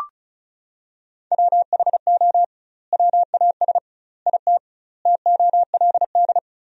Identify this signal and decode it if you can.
Morse code, heard as a keyed tone, about 36 wpm, 700 Hz. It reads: WHO WAS IT TOLD
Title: 2x Word-Spacing - Sentences from Top 400 Words - 35wpm